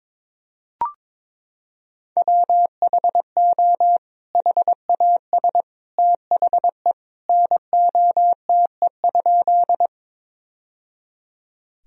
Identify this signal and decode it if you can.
Morse code, heard as a keyed tone, about 22 wpm, 700 Hz. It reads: WHO HAS THE NOTE?
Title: Sentences from Top 1000 Words - Review Part 2 - 22wpm